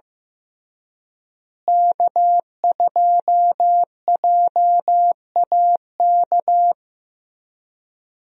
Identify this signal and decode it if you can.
Morse code, heard as a keyed tone, about 15 wpm, 700 Hz. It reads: K2JAK
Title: USA Call Signs - 5 Characters - Repeated 2x - 15wpm